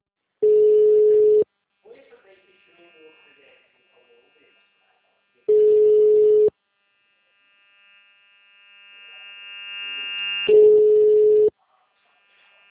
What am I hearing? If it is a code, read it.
Ringback tone (Europe, 425 Hz), with speech.